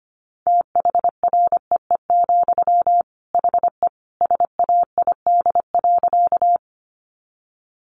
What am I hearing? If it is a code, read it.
Morse code, heard as a keyed tone, about 25 wpm, 700 Hz. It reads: THREE, HE SAID.
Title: Sentences from Top 200 Words 25wpm